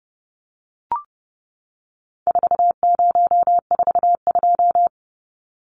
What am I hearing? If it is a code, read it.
Morse code, heard as a keyed tone, about 30 wpm, 700 Hz. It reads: 4042
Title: Numbers - 4-Digit 30wpm